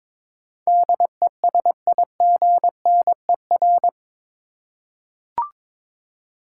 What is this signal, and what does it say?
Morse code, heard as a keyed tone, about 22 wpm, 700 Hz. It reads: DESIGNER